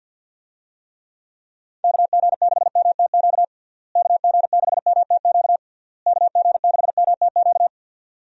Morse code, heard as a keyed tone, about 50 wpm, 700 Hz. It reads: KG6MTQ KG6MTQ KG6MTQ
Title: USA Call Signs - 6 Characters - Repeated 3x - 50wpm